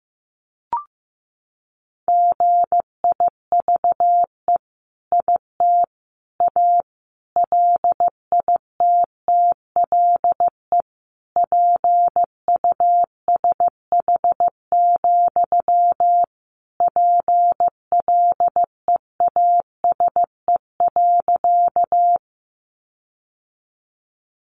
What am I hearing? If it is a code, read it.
Morse code, heard as a keyed tone, about 15 wpm, 700 Hz. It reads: GIVE IT A LITTLE PUSH, PLEASE.
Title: Sentences from Top 1500 Words 15wpm